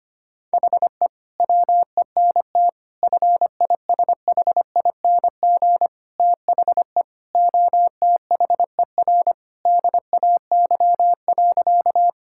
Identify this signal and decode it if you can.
Morse code, heard as a keyed tone, about 25 wpm, 700 Hz. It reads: HE WENT FISHING THE OTHER DAY.
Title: Sentences from Top 1000 Words - Review Part 5 - 25wpm